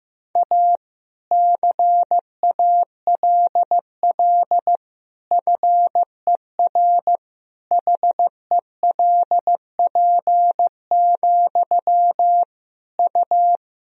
Morse code, heard as a keyed tone, about 15 wpm, 700 Hz. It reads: A CALL FER HELP, U
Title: Sentences from Top 300 Words and Abbreviations 15wpm